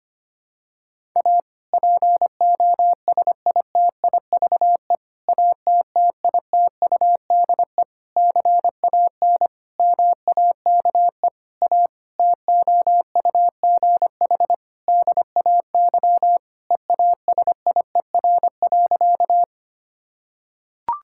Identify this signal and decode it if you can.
Morse code, heard as a keyed tone, about 25 wpm, 700 Hz. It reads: A POSITIVE ATTITUDE CAN MAKE A TOUGH DAY EASIER.
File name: practice / 25wpm / Difficult Sentences with Hints - 25wpm